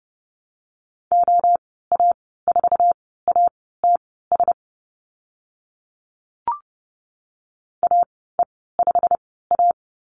Morse code, heard as a keyed tone, about 30 wpm, spaced as for 17 wpm, 700 Hz. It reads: OA4ATS AE5A
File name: Lesson 15g - Number 5 - Call Signs 30wpm